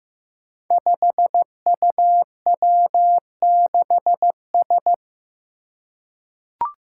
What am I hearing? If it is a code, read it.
Morse code, heard as a keyed tone, about 15 wpm, 700 Hz. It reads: 5UW6S
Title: Five Challenging Letter-Number Combinations 15wpm